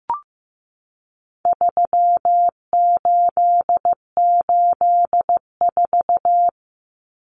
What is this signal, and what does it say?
Morse code, heard as a keyed tone, about 15 wpm, 700 Hz. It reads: 3884
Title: Common Cut Numbers - 4-Digit 15wpm